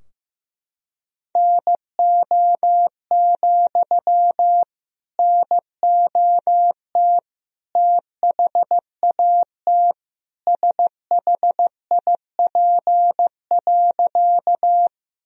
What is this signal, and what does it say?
Morse code, heard as a keyed tone, about 15 wpm, 700 Hz. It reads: NO, NOT THAT SHIP.
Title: Sentences from Top 1000 Words - Review Part 2 - 15wpm